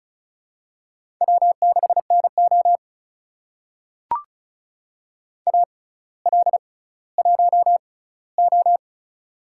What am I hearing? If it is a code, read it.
Morse code, heard as a keyed tone, about 36 wpm, spaced as for 18 wpm, 700 Hz. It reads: W6NO AL1O
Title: USA Call Signs - 4 Characters - 6x Character Spacing - 35wpm